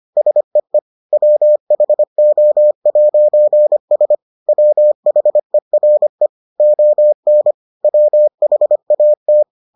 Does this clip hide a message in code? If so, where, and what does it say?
Yes: Morse code from 0.2 to 9.4 s, heard as a keyed tone, about 25 wpm, 600 Hz. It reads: SEE WHO'S WHERE ON WHAT